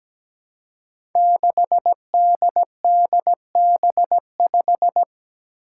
Morse code, heard as a keyed tone, about 17 wpm, 700 Hz. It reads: 6DDB5
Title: Five Challenging Letter-Number Combinations 17wpm